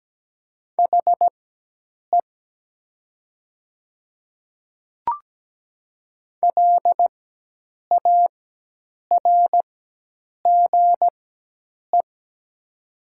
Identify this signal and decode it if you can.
Morse code, heard as a keyed tone, about 17 wpm, spaced as for 8 wpm, 700 Hz. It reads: HE LARGE
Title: 4x Character Spacing - Top 300 Words - 17wpm